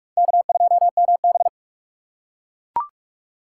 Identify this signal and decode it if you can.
Morse code, heard as a keyed tone, about 44 wpm, 700 Hz. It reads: K2MB